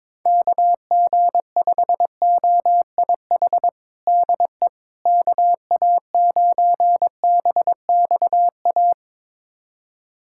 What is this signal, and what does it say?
Morse code, heard as a keyed tone, about 22 wpm, 700 Hz. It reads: KG5OIH DE KA9BXA
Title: USA Callsign DE Callsign - 6 Characters - 22wpm